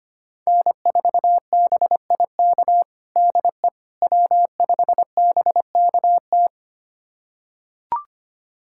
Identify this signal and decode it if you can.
Morse code, heard as a keyed tone, about 25 wpm, 700 Hz. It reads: N4BIK DE W5BKT
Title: USA Callsign DE Callsign - 5 Characters - 25wpm